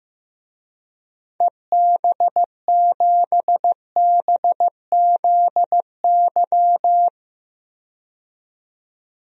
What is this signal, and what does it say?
Morse code, heard as a keyed tone, about 15 wpm, 700 Hz. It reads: EB7BZY
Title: DXCC-100 Call Signs - 6 Characters - 15wpm